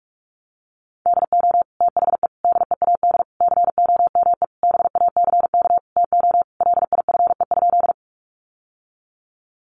Morse code, heard as a keyed tone, about 44 wpm, 700 Hz. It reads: DO THE DEAD COME BACK TO LIFE?
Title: Sentences from Top 1500 Words 45wpm